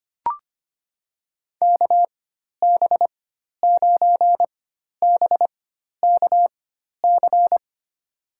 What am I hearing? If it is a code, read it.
Morse code, heard as a keyed tone, about 25 wpm, spaced as for 12 wpm, 700 Hz. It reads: KB9BKC
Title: USA Call Signs - 6 Characters - 4x Character Spacing - 25wpm